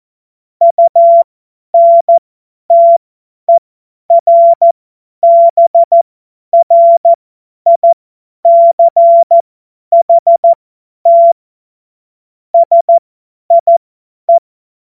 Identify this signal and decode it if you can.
Morse code, heard as a keyed tone, about 14 wpm, spaced as for 10 wpm, 680 Hz. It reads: UNTERBRICHT SIE
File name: Morsekurs / Witze / Witz 12 März 2024 50 F70